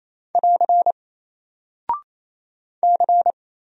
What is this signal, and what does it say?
Morse code, heard as a keyed tone, about 27 wpm, 700 Hz. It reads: <AR> C
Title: Single Character (Letter:Number:Punctuation:Prosign) 28wpm